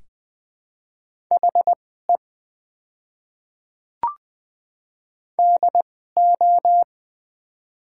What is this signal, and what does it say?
Morse code, heard as a keyed tone, about 20 wpm, spaced as for 14 wpm, 700 Hz. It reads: HE DO
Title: 2x Character Spacing - Top 300 Words - 20wpm